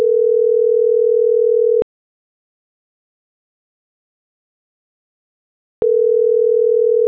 Ringback tone (North America), 440 + 480 Hz.